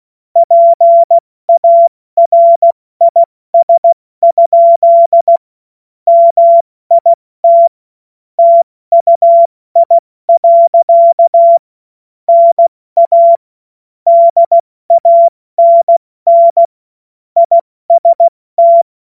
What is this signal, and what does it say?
Morse code, heard as a keyed tone, about 16 wpm, spaced as for 14 wpm, 680 Hz. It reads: PARIS? MIT TUI. NA DANN IST
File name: Morsekurs / Witze / Witz 19. Oktober 2021, 70 F80